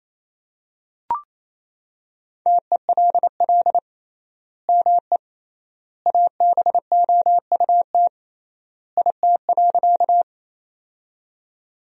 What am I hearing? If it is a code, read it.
Morse code, heard as a keyed tone, about 28 wpm, 700 Hz. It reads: TELL ME ABOUT IT.
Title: 3x Word-Spacing - Sentences from Top 200 Words - 28wpm